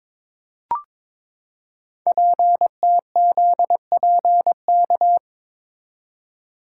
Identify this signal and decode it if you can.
Morse code, heard as a keyed tone, about 22 wpm, 700 Hz. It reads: PTZPK
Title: Code Groups - 5 Letters - 22wpm